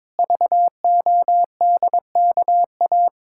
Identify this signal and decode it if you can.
Morse code, heard as a keyed tone, about 22 wpm, 700 Hz. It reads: VODKA
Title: Pangram Challenge v2 - 22wpm